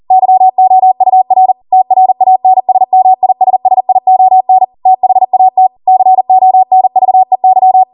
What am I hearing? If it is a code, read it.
Morse code, heard as a keyed tone, about 40 wpm, 750 Hz. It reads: YOUR TRANSMISSION THAT CONVEY